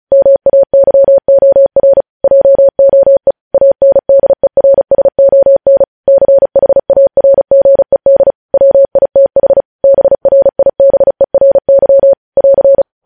Morse code, heard as a keyed tone, about 36 wpm, 550 Hz. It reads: MAYOR JOE ANDERSON CHARGED WITH BRIBERY <AR>
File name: podcast / News headlines-1x-35x35